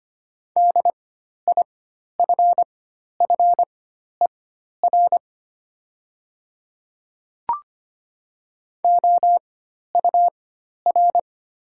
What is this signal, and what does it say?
Morse code, heard as a keyed tone, about 25 wpm, spaced as for 12 wpm, 700 Hz. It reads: DIFFER OUR
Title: 4x Character Spacing - Top 300 Words - 25wpm